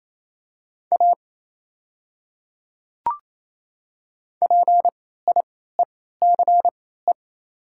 Morse code, heard as a keyed tone, about 27 wpm, spaced as for 16 wpm, 700 Hz. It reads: A PIECE
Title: 3x Character Spacing - Top 500 Words - 28wpm